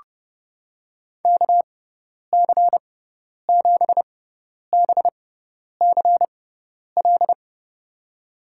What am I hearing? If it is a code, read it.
Morse code, heard as a keyed tone, about 32 wpm, spaced as for 10 wpm, 700 Hz. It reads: KC7BCL
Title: USA Call Signs - 6x Character Spacing - 30wpm